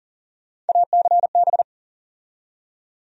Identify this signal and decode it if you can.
Morse code, heard as a keyed tone, about 40 wpm, 700 Hz. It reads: ACB
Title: USA Call Sign Postfixes - 3 Letters 40wpm